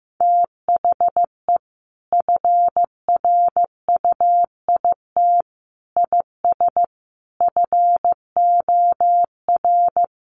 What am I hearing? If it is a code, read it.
Morse code, heard as a keyed tone, about 15 wpm, 700 Hz. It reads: THE FRUIT IS FOR
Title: Sentences from Top 1300 Words 15wpm